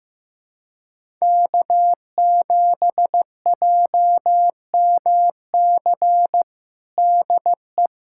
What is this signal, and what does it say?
Morse code, heard as a keyed tone, about 15 wpm, 700 Hz. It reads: K7JMC DE
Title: USA Callsign DE Callsign 15wpm